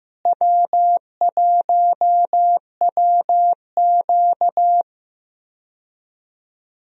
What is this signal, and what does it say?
Morse code, heard as a keyed tone, about 15 wpm, 700 Hz. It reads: W1WQ
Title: USA Call Signs - 4 Characters - Repeated 2x - 15wpm